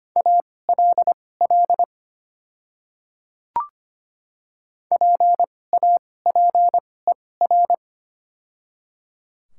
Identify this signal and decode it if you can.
Morse code, heard as a keyed tone, about 25 wpm, spaced as for 18 wpm, 700 Hz. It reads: ALL PAPER
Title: 2x Character Spacing - Top 500 Words - 25wpm